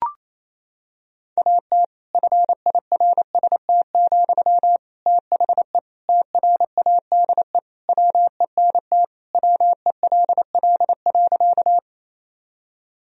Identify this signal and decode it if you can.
Morse code, heard as a keyed tone, about 27 wpm, 700 Hz. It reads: AT FIRST, THE TRADE WENT WELL.
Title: Sentences from Top 1000 Words - Review Part 4 - 28wpm